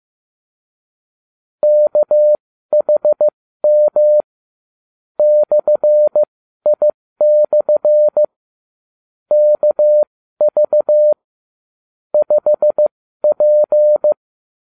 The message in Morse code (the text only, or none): KHM /I/ KV 5P